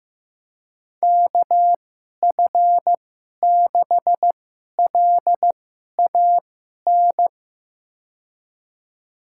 Morse code, heard as a keyed tone, about 15 wpm, spaced as for 11 wpm, 700 Hz. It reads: KF6LAN
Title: USA Call Signs - 6 Characters - 2x Character Spacing - 15wpm